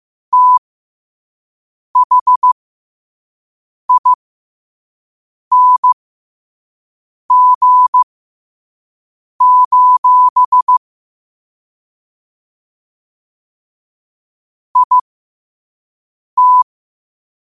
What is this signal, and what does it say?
Morse code, heard as a keyed tone, about 15 wpm, spaced as for 5 wpm, 1000 Hz. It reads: THING: IT
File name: qotdfiles / 5wpm